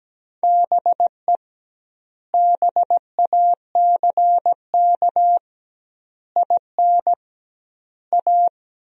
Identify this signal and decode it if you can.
Morse code, heard as a keyed tone, about 17 wpm, 700 Hz. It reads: BE BACK IN A
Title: 2x Word-Spacing - Sentences from Top 500 Words - 17wpm